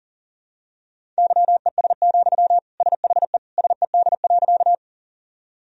Morse code, heard as a keyed tone, about 40 wpm, 700 Hz. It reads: YES, SHE SED.
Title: Sentences from Top 500 Words and Abbreviations 40wpm